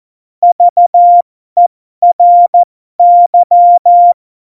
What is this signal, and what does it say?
Morse code, heard as a keyed tone, about 14 wpm, spaced as for 12 wpm, 700 Hz. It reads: VERY
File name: Morsekurs / Witze